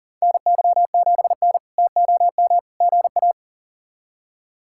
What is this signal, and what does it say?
Morse code, heard as a keyed tone, about 39 wpm, 700 Hz. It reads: NY7N TOM GA
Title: K1USN SST QSO Practice - 40wpm